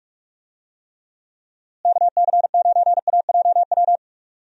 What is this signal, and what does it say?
Morse code, heard as a keyed tone, about 41 wpm, 700 Hz. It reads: KC9AJW
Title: USA Call Signs - 6 Characters - Repeated 3x - 45wpm